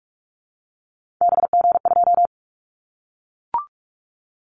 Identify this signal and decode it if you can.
Morse code, heard as a keyed tone, about 44 wpm, 700 Hz. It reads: BG2
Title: Callsign Prefixes for Top 100 DXCC - 45wpm